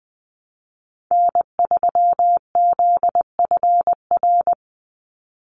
Morse code, heard as a keyed tone, about 20 wpm, 700 Hz. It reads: N3ZFR
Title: USA Call Signs 20wpm